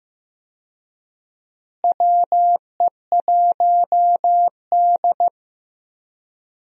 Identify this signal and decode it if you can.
Morse code, heard as a keyed tone, about 15 wpm, 700 Hz. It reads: WE1D
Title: USA Call Signs - 4 Characters - Repeated 3x - 15wpm